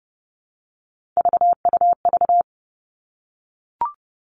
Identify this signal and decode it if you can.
Morse code, heard as a keyed tone, about 30 wpm, 700 Hz. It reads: VUV
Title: Three Challenging Letter-Number Combinations 30wpm